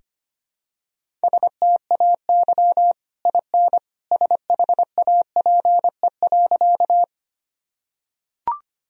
Morse code, heard as a keyed tone, about 25 wpm, 700 Hz. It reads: STAY IN SHAPE.
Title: Sentences from Top 1000 Words - Review Part 3 - 25wpm